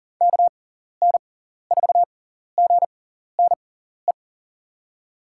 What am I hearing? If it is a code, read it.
Morse code, heard as a keyed tone, about 40 wpm, spaced as for 14 wpm, 700 Hz. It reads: KN4GNE
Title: USA Call Signs - 6x Character Spacing - 40wpm